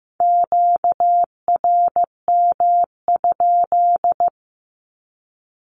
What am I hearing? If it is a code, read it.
Morse code, heard as a keyed tone, about 15 wpm, 700 Hz. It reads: QRM?
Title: Q-Codes 15wpm